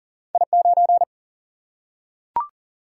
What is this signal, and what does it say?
Morse code, heard as a keyed tone, about 40 wpm, 700 Hz. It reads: I9
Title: Single-Letter Single-Number 40wpm